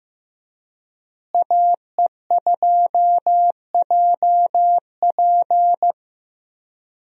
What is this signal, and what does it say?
Morse code, heard as a keyed tone, about 15 wpm, 700 Hz. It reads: AE2JP
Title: USA Call Signs - 5 Characters - Repeated 3x - 15wpm